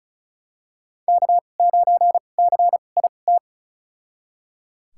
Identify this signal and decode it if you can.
Morse code, heard as a keyed tone, about 36 wpm, spaced as for 25 wpm, 700 Hz. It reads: K9CIT
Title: USA Call Signs - 5 Characters - 2x Character Spacing - 35wpm